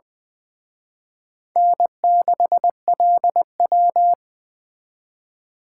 Morse code, heard as a keyed tone, about 23 wpm, 700 Hz. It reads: N6LW
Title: Speed Racing - USA 1x2 Call Signs 20wpm